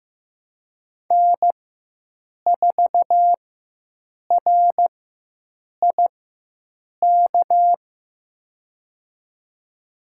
Morse code, heard as a keyed tone, about 15 wpm, spaced as for 7 wpm, 700 Hz. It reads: N4RIK
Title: USA Call Signs - 5 Characters - 4x Character Spacing - 15wpm